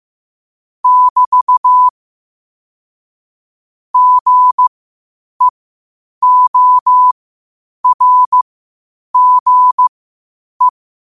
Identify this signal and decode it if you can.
Morse code, heard as a keyed tone, about 15 wpm, spaced as for 8 wpm, 1000 Hz. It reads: <BT> GEORGE
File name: qotdfiles / 8wpm